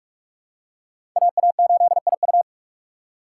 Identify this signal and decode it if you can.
Morse code, heard as a keyed tone, about 43 wpm, 700 Hz. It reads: AA8IU
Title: USA Call Signs - 2x Character Spacing - 45wpm